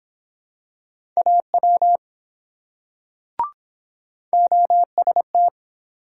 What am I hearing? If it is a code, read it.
Morse code, heard as a keyed tone, about 23 wpm, 700 Hz. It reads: AW OST